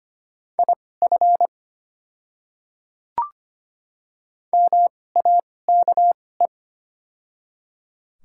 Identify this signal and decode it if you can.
Morse code, heard as a keyed tone, about 25 wpm, spaced as for 18 wpm, 690 Hz. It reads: IF MAKE
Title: 2x Character Spacing - Top 100 Words - 25wpm